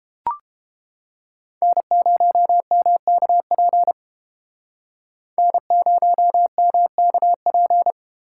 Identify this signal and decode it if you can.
Morse code, heard as a keyed tone, about 31 wpm, 700 Hz. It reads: N0MKP N0MKP